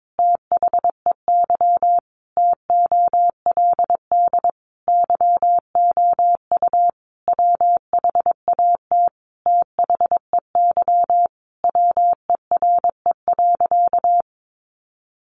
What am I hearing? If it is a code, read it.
Morse code, heard as a keyed tone, about 22 wpm, 700 Hz. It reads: THEY TOLD YOU WHAT THEY WERE.
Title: Sentences from Top 1000 Words 22wpm